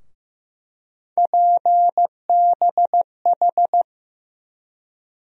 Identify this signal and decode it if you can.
Morse code, heard as a keyed tone, about 15 wpm, 700 Hz. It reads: PBH
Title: USA Call Sign Postfixes - 3 Letters 15wpm